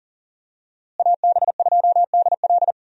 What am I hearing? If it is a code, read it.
Morse code, heard as a keyed tone, about 37 wpm, 700 Hz. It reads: AB2DL
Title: USA Call Signs - 5 Characters - Repeated 2x - 40wpm